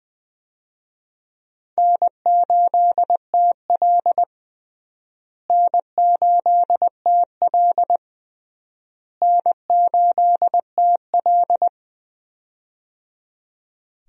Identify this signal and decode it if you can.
Morse code, heard as a keyed tone, about 21 wpm, 700 Hz. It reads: N8TL N8TL N8TL
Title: USA Call Signs - 4 Characters - Repeated 3x - 20wpm